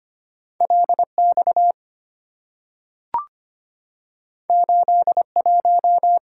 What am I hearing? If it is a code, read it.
Morse code, heard as a keyed tone, about 25 wpm, 690 Hz. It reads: LX 81
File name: Two Letter-Number Combinations - 25wpm